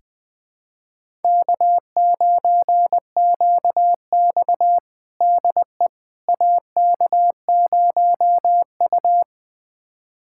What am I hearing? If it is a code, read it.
Morse code, heard as a keyed tone, about 21 wpm, 700 Hz. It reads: K9QX DE AK0U
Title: USA Callsign DE Callsign - 4 Characters - 20wpm